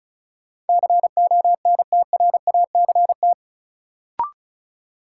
Morse code, heard as a keyed tone, about 34 wpm, 700 Hz. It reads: CONTRACT
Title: Sentences from Top 1400 Words - New Word Drill - 35wpm